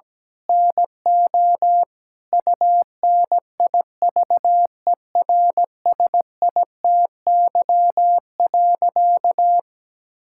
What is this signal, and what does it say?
Morse code, heard as a keyed tone, about 17 wpm, 700 Hz. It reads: NO UNIVERSITY.
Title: Sentences from Top 1000 Words - Review Part 5 - 17wpm